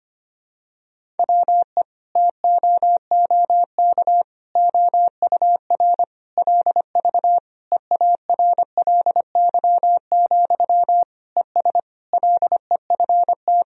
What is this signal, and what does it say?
Morse code, heard as a keyed tone, about 25 wpm, 700 Hz. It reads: WE TOOK OUR LV EARLY, ES LEFT
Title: Sentences from Top 400 Words and Abbreviations 25wpm